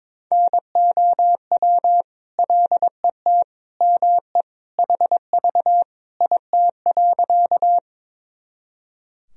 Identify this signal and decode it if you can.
Morse code, heard as a keyed tone, about 22 wpm, 700 Hz. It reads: NOW LET ME HV IT.
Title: Sentences from Top 300 Words and Abbreviations 22wpm